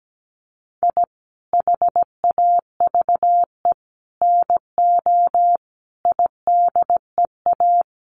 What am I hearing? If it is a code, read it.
Morse code, heard as a keyed tone, about 17 wpm, 700 Hz. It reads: I HAVE NO IDEA
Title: Sets of 5 Words Encore 17wpm